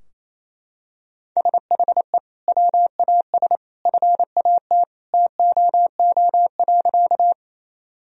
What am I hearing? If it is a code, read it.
Morse code, heard as a keyed tone, about 27 wpm, 700 Hz. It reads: SHE WAS FAT TOO.
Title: Sentences from Top 1000 Words - Review Part 4 - 28wpm